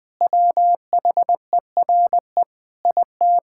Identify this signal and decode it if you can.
Morse code, heard as a keyed tone, about 20 wpm, 700 Hz. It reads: WHERE IT
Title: Sentences with 4-Word Phrases - Spoken Beginning - 20wpm